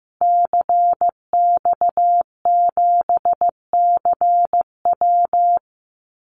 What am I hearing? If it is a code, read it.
Morse code, heard as a keyed tone, about 15 wpm, 700 Hz. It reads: CX7CW
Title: Call Signs Repeated 3x - 15wpm